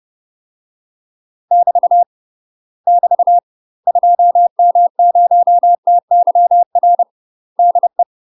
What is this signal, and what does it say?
Morse code, heard as a keyed tone, about 30 wpm, 700 Hz. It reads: <BT> <BT> 2M0TYR DE
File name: UK MASTER QSOs TRAINING / 30WPM / UK MASTER QSOs 0049